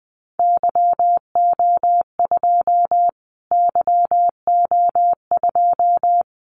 Morse code, heard as a keyed tone, about 20 wpm, 700 Hz. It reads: YO2 YO2
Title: Callsign Prefixes for Top 25 DXCC - Repeated 3x - 20wpm